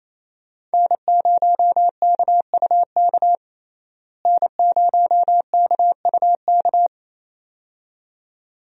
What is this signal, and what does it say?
Morse code, heard as a keyed tone, about 28 wpm, 700 Hz. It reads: N0KUK N0KUK
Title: USA Call Signs - 5 Characters - Repeated 2x - 28wpm